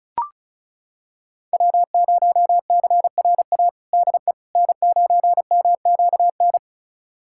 Morse code, heard as a keyed tone, about 36 wpm, 690 Hz. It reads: W0CRA DE N9MQN
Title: USA Callsign DE Callsign - 5 Characters - 35wpm